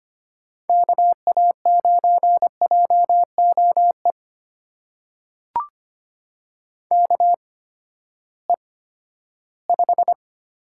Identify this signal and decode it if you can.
Morse code, heard as a keyed tone, about 25 wpm, spaced as for 14 wpm, 700 Hz. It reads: KA9JOE KE5
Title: USA Call Signs - 6 Characters - 8x Character Spacing - 25wpm